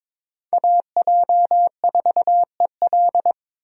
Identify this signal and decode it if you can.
Morse code, heard as a keyed tone, about 22 wpm, 700 Hz. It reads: AJ4EL
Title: USA Call Signs - 5 Characters - 22wpm